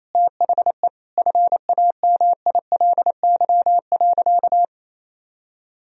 Morse code, heard as a keyed tone, about 28 wpm, 700 Hz. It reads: THE FAMILY.
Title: Sentences from Top 1000 Words - Review Part 4 - 28wpm